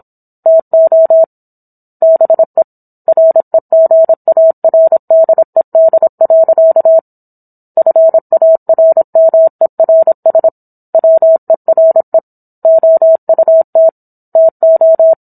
Morse code, heard as a keyed tone, about 27 wpm, 650 Hz. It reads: TO BE REGARDED. FARMERS WERE OUT TO